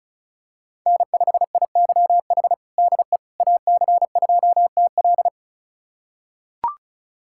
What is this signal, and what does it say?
Morse code, heard as a keyed tone, about 36 wpm, 700 Hz. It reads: N5IYH DE AC2TL
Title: USA Callsign DE Callsign - 5 Characters - 35wpm